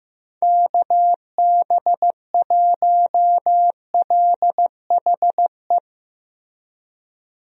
Morse code, heard as a keyed tone, about 15 wpm, 700 Hz. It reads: KB1LHE
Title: USA Call Signs - 6 Characters - Repeated 2x - 15wpm